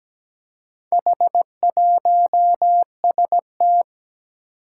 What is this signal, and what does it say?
Morse code, heard as a keyed tone, about 17 wpm, 700 Hz. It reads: H1ST
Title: Four Challenging Letter-Number Combinations 17wpm